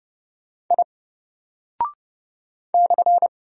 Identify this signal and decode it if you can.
Morse code, heard as a keyed tone, about 30 wpm, 700 Hz. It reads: I /
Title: Single Character (Letter:Number:Punctuation:Prosign) 30wpm